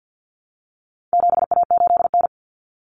Morse code, heard as a keyed tone, about 50 wpm, 700 Hz. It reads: 7A8N
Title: Common Cut Numbers - 4-Digit 50wpm